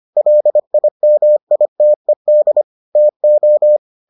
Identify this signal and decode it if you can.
Morse code, heard as a keyed tone, about 25 wpm, 600 Hz. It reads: LIMITED TO